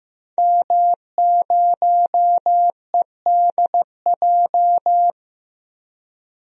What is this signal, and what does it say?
Morse code, heard as a keyed tone, about 15 wpm, 700 Hz. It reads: M0EDJ
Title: DXCC-100 Call Signs - 5 Characters - Repeated 3x - 15wpm